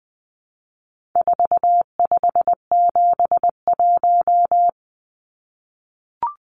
Morse code, heard as a keyed tone, about 20 wpm, 700 Hz. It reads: 4571
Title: Numbers - 4-Digit 20wpm